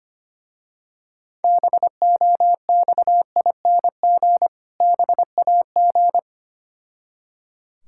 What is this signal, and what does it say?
Morse code, heard as a keyed tone, about 25 wpm, 690 Hz. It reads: BOXING BAG
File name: practice / 25wpm / Difficult Words with Hints - 25wpm